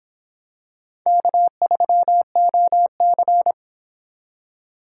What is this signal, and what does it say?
Morse code, heard as a keyed tone, about 26 wpm, 700 Hz. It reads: K3OC